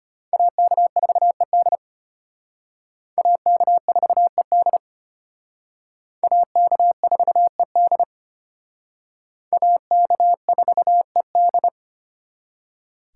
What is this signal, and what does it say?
Morse code, heard as a keyed tone, about 32 wpm, 700 Hz. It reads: AK4ED AK4ED AK4ED AK4ED